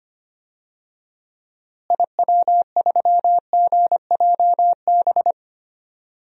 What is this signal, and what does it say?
Morse code, heard as a keyed tone, about 25 wpm, 700 Hz. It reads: IW3GJB
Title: DXCC-100 Call Signs - 6 Characters - Repeated 2x - 25wpm